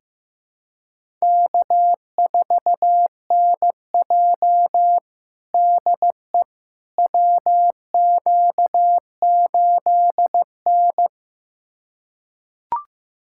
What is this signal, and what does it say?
Morse code, heard as a keyed tone, about 16 wpm, 700 Hz. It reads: K4NJ DE WQ8N
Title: USA Callsign DE Callsign - 4 Characters - 15wpm